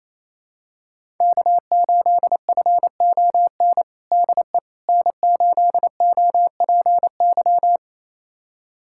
Morse code, heard as a keyed tone, about 28 wpm, 700 Hz. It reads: K8FON DE N8OPY
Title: USA Callsign DE Callsign - 5 Characters - 28wpm